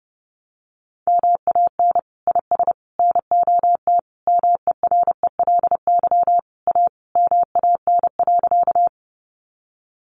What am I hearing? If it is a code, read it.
Morse code, heard as a keyed tone, about 30 wpm, 700 Hz. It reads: MAN IS NOT MERELY A MAN.
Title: Sentences from Top 1400 Words 30wpm